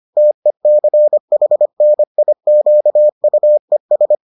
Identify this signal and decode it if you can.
Morse code, heard as a keyed tone, about 25 wpm, 600 Hz. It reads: TECHNIQUES